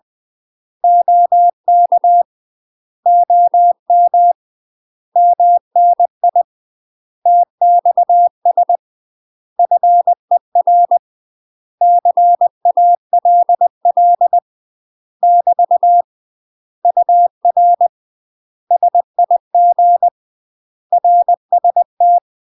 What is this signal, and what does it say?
Morse code, heard as a keyed tone, about 20 wpm, 700 Hz. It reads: OK OM MNI TXS FER CALL <BT> UR SIG RST
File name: UK MASTER QSOs TRAINING/20WPM